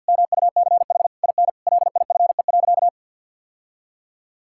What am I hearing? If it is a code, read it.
Morse code, heard as a keyed tone, about 47 wpm, 700 Hz. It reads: MUCH IN LIFE.
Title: Sentences from Top 1000 Words - Review Part 6 - 50wpm